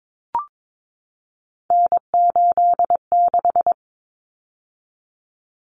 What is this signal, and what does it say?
Morse code, heard as a keyed tone, about 22 wpm, 700 Hz. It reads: N86
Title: Common Cut Numbers - 3-Digit 22wpm